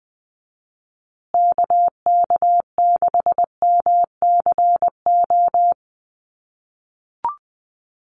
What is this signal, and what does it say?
Morse code, heard as a keyed tone, about 20 wpm, 700 Hz. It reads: KK6MCO
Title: USA Call Signs Repeated 3x 20wpm